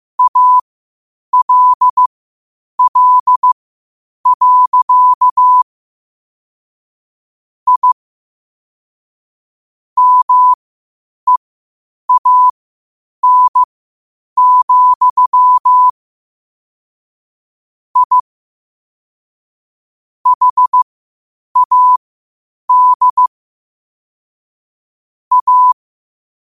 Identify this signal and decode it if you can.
Morse code, heard as a keyed tone, about 15 wpm, spaced as for 8 wpm, 1000 Hz. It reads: ALL. I MEAN, I HAD A
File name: qotdfiles / 8wpm